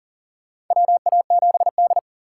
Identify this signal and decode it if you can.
Morse code, heard as a keyed tone, about 40 wpm, 700 Hz. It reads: WA7D
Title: USA Call Signs - 4 Characters - 2x Character Spacing - 40wpm